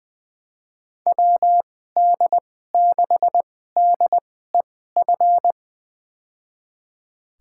Morse code, heard as a keyed tone, about 20 wpm, spaced as for 14 wpm, 700 Hz. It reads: WD6DEF
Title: USA Call Signs - 2x Character Spacing - 20wpm